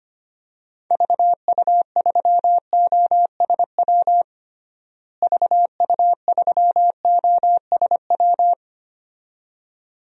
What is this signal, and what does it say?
Morse code, heard as a keyed tone, about 25 wpm, 700 Hz. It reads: VU3OSW VU3OSW
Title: DXCC-100 Call Signs - 6 Characters - Repeated 2x - 25wpm